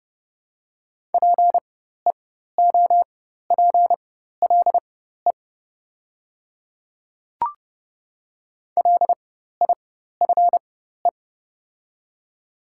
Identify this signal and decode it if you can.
Morse code, heard as a keyed tone, about 30 wpm, spaced as for 14 wpm, 700 Hz. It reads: PEOPLE LIFE
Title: 4x Character Spacing - Top 300 Words - 30wpm